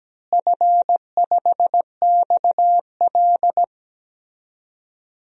Morse code, heard as a keyed tone, about 17 wpm, 700 Hz. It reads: F5XL
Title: DXCC-100 Call Signs - 4 Characters - Repeated 3x - 17wpm